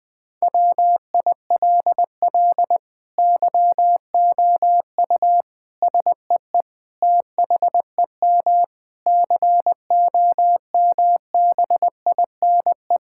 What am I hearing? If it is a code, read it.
Morse code, heard as a keyed tone, about 20 wpm, 700 Hz. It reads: WILL YOU SEE THEM COMBINE
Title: Sentences from Top 1600 Words 20wpm